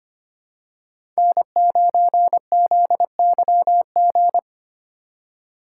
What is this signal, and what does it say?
Morse code, heard as a keyed tone, about 25 wpm, 700 Hz. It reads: N9ZYG
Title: USA Call Signs - 2x Character Spacing - 25wpm